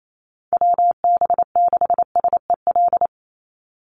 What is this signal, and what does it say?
Morse code, heard as a keyed tone, about 28 wpm, 690 Hz. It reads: WB6SEL
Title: Call Signs Repeated 3x - 28wpm